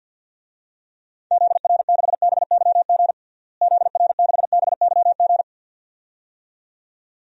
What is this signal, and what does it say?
Morse code, heard as a keyed tone, about 52 wpm, 700 Hz. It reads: ZR6BYG ZR6BYG
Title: DXCC-100 Call Signs - 6 Characters - Repeated 2x - 50wpm